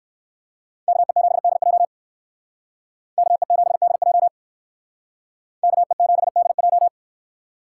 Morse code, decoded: KE7DJ KE7DJ KE7DJ

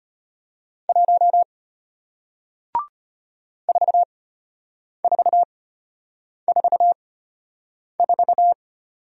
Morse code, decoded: 1 4444

